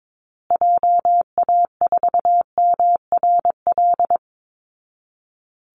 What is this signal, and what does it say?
Morse code, heard as a keyed tone, about 22 wpm, 700 Hz. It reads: JA4MRL
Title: Call Signs Repeated 3x - 22wpm